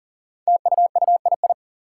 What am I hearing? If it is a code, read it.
Morse code, heard as a keyed tone, about 38 wpm, 700 Hz. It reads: TUUII
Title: Five Challenging Letter-Number Combinations 40wpm